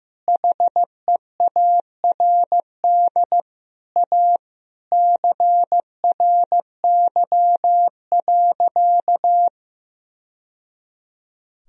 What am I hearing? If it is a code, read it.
Morse code, heard as a keyed tone, about 15 wpm, 700 Hz. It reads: HEARD A CRY.
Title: Sentences from Top 1000 Words - Review Part 2 - 15wpm